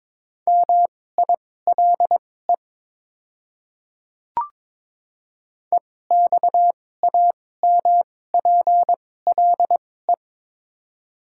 Morse code, heard as a keyed tone, about 22 wpm, spaced as for 16 wpm, 700 Hz. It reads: MILE EXAMPLE
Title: 2x Character Spacing - Top 300 Words - 22wpm